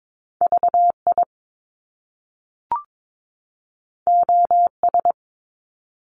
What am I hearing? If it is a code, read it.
Morse code, heard as a keyed tone, about 22 wpm, 700 Hz. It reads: VI OS
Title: Top Two Letter Combinations in Words 22wpm